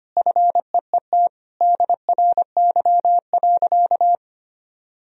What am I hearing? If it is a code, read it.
Morse code, heard as a keyed tone, about 25 wpm, 700 Hz. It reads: FEET DRY.
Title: Sentences from Top 1000 Words - Review Part 3 - 25wpm